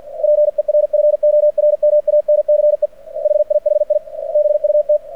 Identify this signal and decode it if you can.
Morse code, heard as a keyed tone, about 49 wpm, 600 Hz. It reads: PERFORMANCE HIHI BUT
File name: archived / CRSnet